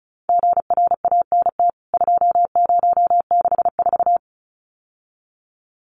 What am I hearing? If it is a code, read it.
Morse code, heard as a keyed tone, about 34 wpm, 700 Hz. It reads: GRANT 2064
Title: CWT Exchange 35wpm